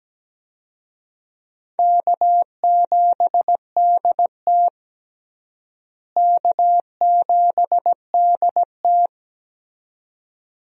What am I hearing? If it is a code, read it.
Morse code, heard as a keyed tone, about 17 wpm, 700 Hz. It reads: K7DT K7DT
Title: USA Call Signs - 4 Characters - Repeated 2x - 17wpm